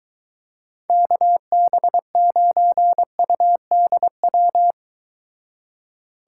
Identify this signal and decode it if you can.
Morse code, heard as a keyed tone, about 22 wpm, 700 Hz. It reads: KB9UDW